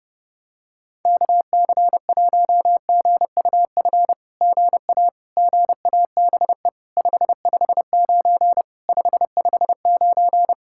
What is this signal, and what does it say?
Morse code, heard as a keyed tone, about 29 wpm, 700 Hz. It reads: KC1GUF GA GABE 559 559
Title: SOTA QSO Practice 30wpm